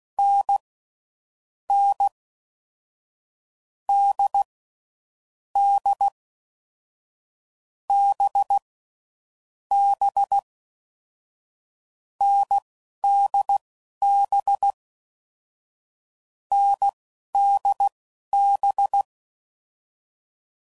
Morse code, heard as a keyed tone, about 16 wpm, spaced as for 11 wpm, 800 Hz. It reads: N N D D B B NDB NDB
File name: Mixed / Short Runs / awjndb6gz7